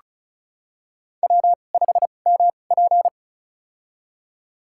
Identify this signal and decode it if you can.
Morse code, heard as a keyed tone, about 34 wpm, spaced as for 25 wpm, 700 Hz. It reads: W5MP
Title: USA Call Signs - 4 Characters - 2x Character Spacing - 35wpm